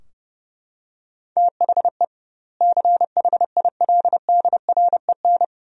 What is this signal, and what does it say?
Morse code, heard as a keyed tone, about 30 wpm, 700 Hz. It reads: THE CHILDREN
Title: 2x Word-Spacing - Sentences from Top 300 Words - 30wpm